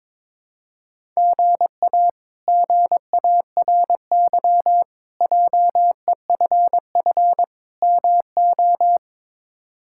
Morse code, heard as a keyed tone, about 21 wpm, 700 Hz. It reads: GA GARY JEFF MO
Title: K1USN SST QSO Practice - 22wpm